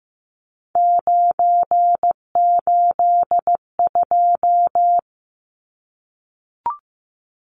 Morse code, heard as a keyed tone, about 15 wpm, 700 Hz. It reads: 982
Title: Numbers - 3-Digit 15wpm